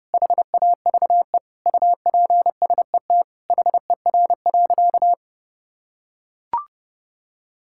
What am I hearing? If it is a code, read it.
Morse code, heard as a keyed tone, about 32 wpm, 700 Hz. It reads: HAVE UPSET HER.
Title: Sentences from Top 1000 Words - Review Part 5 - 30wpm